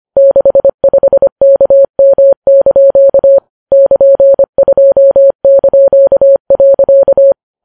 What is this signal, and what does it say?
Morse code, heard as a keyed tone, about 25 wpm, 550 Hz. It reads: 65KM) <KN>2).